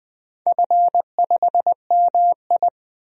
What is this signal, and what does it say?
Morse code, heard as a keyed tone, about 20 wpm, 700 Hz. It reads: F5MI
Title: DXCC-100 Call Signs - 4 Characters - 20wpm